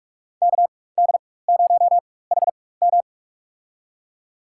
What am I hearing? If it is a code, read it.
Morse code, heard as a keyed tone, about 46 wpm, spaced as for 21 wpm, 700 Hz. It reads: KD0HM
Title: USA Call Signs - 5 Characters - 4x Character Spacing - 45wpm